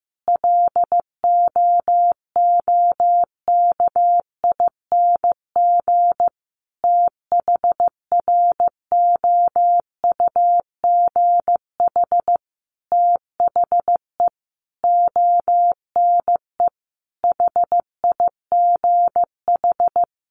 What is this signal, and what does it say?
Morse code, heard as a keyed tone, about 15 wpm, 700 Hz. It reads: LOOKING THROUGH THE ONE HIGH
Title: A Sound Of Thunder 15wpm